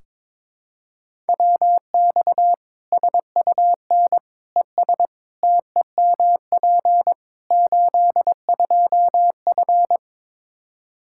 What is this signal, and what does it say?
Morse code, heard as a keyed tone, about 22 wpm, 700 Hz. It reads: WX SUN ES TEMP 82F
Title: K1USN SST QSO Practice 22wpm